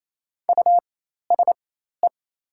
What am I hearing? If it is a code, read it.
Morse code, heard as a keyed tone, about 28 wpm, spaced as for 13 wpm, 700 Hz. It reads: USE
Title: 4x Character Spacing - Top 100 Words - 28wpm